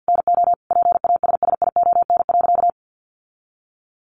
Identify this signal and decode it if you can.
Morse code, heard as a keyed tone, about 50 wpm, 700 Hz. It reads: NO PASSION.
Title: Sentences from Top 1300 Words 50wpm